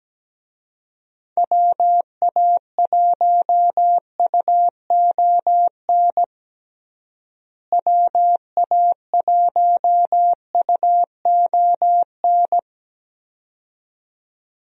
Morse code, heard as a keyed tone, about 17 wpm, 700 Hz. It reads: WA1UON WA1UON
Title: USA Call Signs - 6 Characters - Repeated 2x - 17wpm